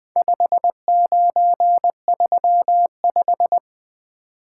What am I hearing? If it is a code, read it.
Morse code, heard as a keyed tone, about 20 wpm, 700 Hz. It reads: 5935
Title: Speed-Racing - Four Letter-Number Combinations - 20wpm